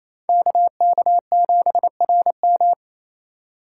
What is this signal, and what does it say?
Morse code, heard as a keyed tone, about 28 wpm, 700 Hz. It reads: KK7RM
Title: USA Call Signs - 5 Characters - 2x Character Spacing - 28wpm